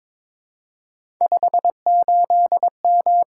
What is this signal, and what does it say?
Morse code, heard as a keyed tone, about 22 wpm, 700 Hz. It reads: 58M
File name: practice / 22wpm / Three Letter-Number Combinations - 22wpm